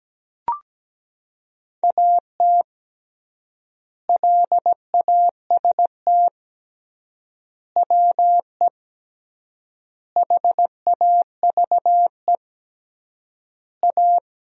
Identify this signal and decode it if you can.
Morse code, heard as a keyed tone, about 17 wpm, 700 Hz. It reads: AT LAST WE HAVE A
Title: 3x Word-Spacing - Sentences from Top 300 Words - 17wpm